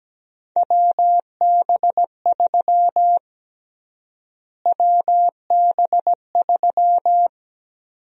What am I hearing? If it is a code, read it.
Morse code, heard as a keyed tone, about 17 wpm, 700 Hz. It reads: WB3 WB3
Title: USA Call Sign Prefixes - Repeated 2x - 17wpm